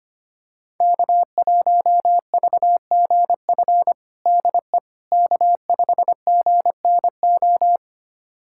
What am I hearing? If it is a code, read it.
Morse code, heard as a keyed tone, about 25 wpm, 700 Hz. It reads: K1VGF DE K5GNO
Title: USA Callsign DE Callsign - 5 Characters - 25wpm